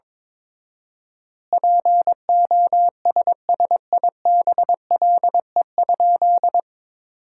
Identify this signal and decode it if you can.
Morse code, heard as a keyed tone, about 21 wpm, 700 Hz. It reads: POSSIBLE?
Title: 4x Word-Spacing - Sentences from Top 500 Words - 22wpm